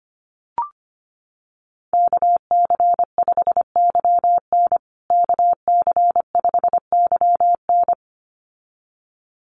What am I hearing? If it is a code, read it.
Morse code, heard as a keyed tone, about 25 wpm, 700 Hz. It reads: KC5YN KC5YN
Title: USA Call Signs Repeated 2x 25wpm